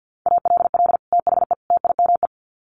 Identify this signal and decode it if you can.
Morse code, heard as a keyed tone, about 50 wpm, 700 Hz. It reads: ALL THE TIME
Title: Sets of 3 Words 50wpm